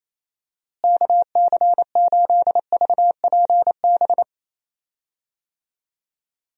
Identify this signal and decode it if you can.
Morse code, heard as a keyed tone, about 28 wpm, 700 Hz. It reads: KC8VPB
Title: USA Call Signs - 6 Characters - Repeated 2x - 28wpm